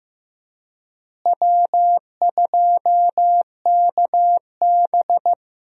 Morse code, heard as a keyed tone, about 15 wpm, 700 Hz. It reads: W2KB
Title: USA Call Signs - 4 Characters - Repeated 2x - 15wpm